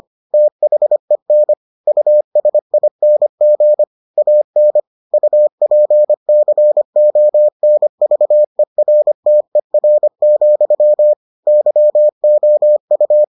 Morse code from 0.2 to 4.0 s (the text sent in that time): THEN USING